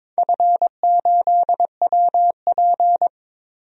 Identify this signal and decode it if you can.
Morse code, heard as a keyed tone, about 23 wpm, 700 Hz. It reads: F8WP
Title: DXCC-100 Call Signs - 4 Characters - 22wpm